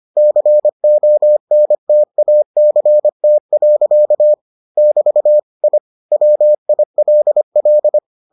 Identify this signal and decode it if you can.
Morse code, heard as a keyed tone, about 25 wpm, 600 Hz. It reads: CONTACT. <BT> I WILL